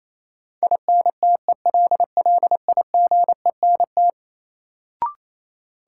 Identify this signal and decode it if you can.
Morse code, heard as a keyed tone, about 28 wpm, 700 Hz. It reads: INTELLIGENT
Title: Sentences from Top 1000 Words - Review Part 7 - New Word Drill - 28wpm